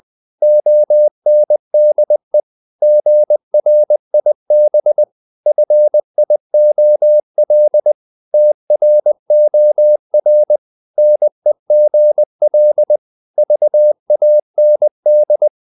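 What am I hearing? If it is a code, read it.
Morse code, heard as a keyed tone, about 20 wpm, 600 Hz. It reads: ONDE GRIB FIOL TROR NEGL VAND
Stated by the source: Morse code practice of short Danish words at 20 wpm, from CW med Gnister.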